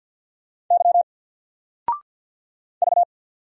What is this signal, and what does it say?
Morse code, heard as a keyed tone, about 50 wpm, 700 Hz. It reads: Y V
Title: Single Letter-Number 50wpm